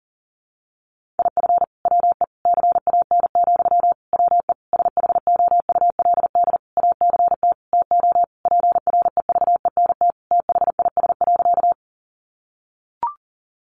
Morse code, heard as a keyed tone, about 40 wpm, 700 Hz. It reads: IF WE CAN, WE SHOULD ACT TO PREVENT THIS.
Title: Sentences from Top 1100 Words 40wpm